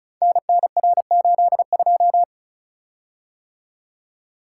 Morse code, heard as a keyed tone, about 36 wpm, 700 Hz. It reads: NNR82
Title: Code Groups - 5 Letter:Numbers - 35wpm